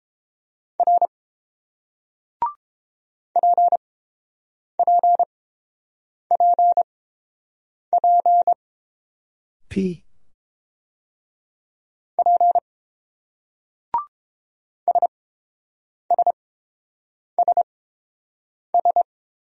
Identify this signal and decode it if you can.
Morse code, heard as a keyed tone, about 29 wpm, spaced as for 7 wpm, 700 Hz. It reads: R PPPP P SSSS